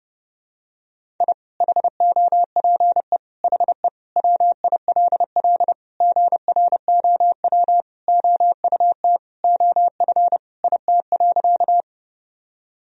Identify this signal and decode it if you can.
Morse code, heard as a keyed tone, about 30 wpm, 700 Hz. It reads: I HOPE HE WILL GROW OUT OF IT.
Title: Sentences from Top 1000 Words - Review Part 1 - 30wpm